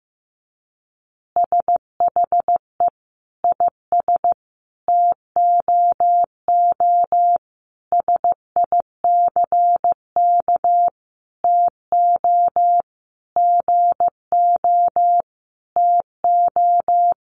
Morse code, heard as a keyed tone, about 15 wpm, 700 Hz. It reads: SHE IS TOO SICK TO GO TO
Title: Easy Phrases 15wpm